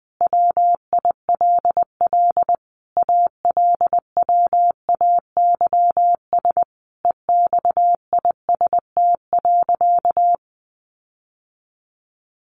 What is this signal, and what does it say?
Morse code, heard as a keyed tone, about 20 wpm, 700 Hz. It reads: WILL ALWAYS EXIST.
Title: Sentences from Top 1500 Words 20wpm